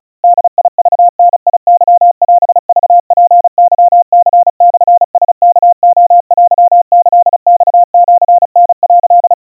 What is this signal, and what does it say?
Morse code (Japanese Wabun code), heard as a keyed tone, about 34 wpm, 700 Hz. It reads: ボクダケオクツケニモラワレテキマシタ」
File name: kawa / Morse sample / kamyu 130 cpm 700 Hz Crispy